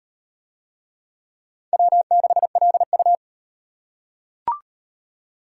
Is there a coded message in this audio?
Yes: Morse code from 1.7 to 3.1 s, heard as a keyed tone, about 37 wpm, 700 Hz. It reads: W6LU